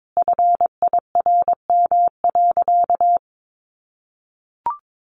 Morse code, heard as a keyed tone, about 22 wpm, 700 Hz. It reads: FIRM.
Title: Sentences from Top 1400 Words 22wpm